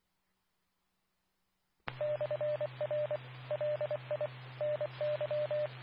Morse code, heard as a keyed tone, about 24 wpm, 600 Hz, with radio noise. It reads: /R LI NY
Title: Nassau and Suffolk County Amateur Repeaters